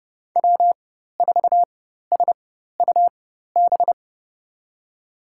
Morse code, heard as a keyed tone, about 30 wpm, spaced as for 14 wpm, 700 Hz. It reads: W4SUB
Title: USA Call Signs - 4x Character Spacing - 30wpm